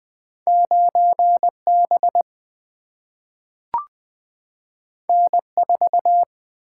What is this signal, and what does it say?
Morse code, heard as a keyed tone, about 20 wpm, 700 Hz. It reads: 9 B N 4
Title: Single-Letter Single-Number 20wpm